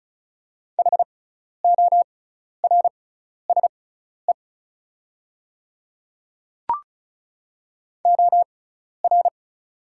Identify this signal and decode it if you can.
Morse code, heard as a keyed tone, about 36 wpm, spaced as for 12 wpm, 690 Hz. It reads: HORSE OR
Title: 6x Character Spacing - Top 300 Words - 35wpm